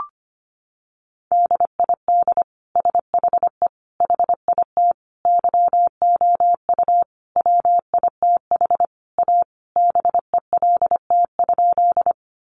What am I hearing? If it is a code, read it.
Morse code, heard as a keyed tone, about 25 wpm, 700 Hz. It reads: DID SHE HIT YOU WITH A BELT?
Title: Sentences from Top 1400 Words 25wpm